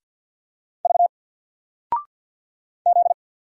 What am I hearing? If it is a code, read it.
Morse code, heard as a keyed tone, about 48 wpm, 700 Hz. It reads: V Z
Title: Single Character (Letter:Number:Punctuation:Prosign) Alt 50wpm